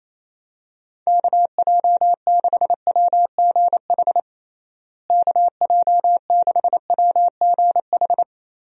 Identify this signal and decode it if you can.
Morse code, heard as a keyed tone, about 28 wpm, 700 Hz. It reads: KJ6WGH KJ6WGH
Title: USA Call Signs - 6 Characters - Repeated 2x - 28wpm